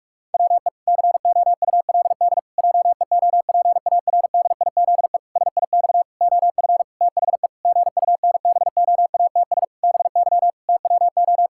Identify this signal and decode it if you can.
Morse code, heard as a keyed tone, about 44 wpm, 700 Hz. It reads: WE COULD JEOPARDIZE SIX OF THE GUNBOATS BY TWO
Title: Pangram Challenge v2 - 45wpm